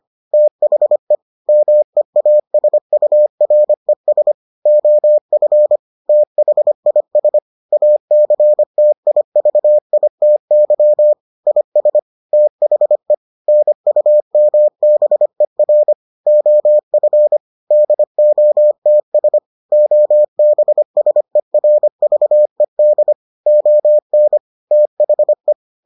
Morse code, heard as a keyed tone, about 25 wpm, 600 Hz. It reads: THE MEASURES OF THIS ACTIVITY IS THE NUMBER OF DOTS OBSERVED ON THE